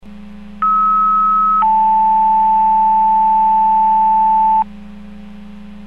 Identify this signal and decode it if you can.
Two-tone page, heard as a one-second tone, then a three-second tone, about 1290 then 880 Hz. It is heard from 0.6 to 4.6 s.